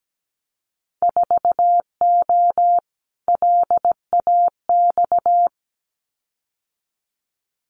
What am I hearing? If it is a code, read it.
Morse code, heard as a keyed tone, about 17 wpm, 700 Hz. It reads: 4O LAX
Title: Winter Field Day Exchanges Encore 17wpm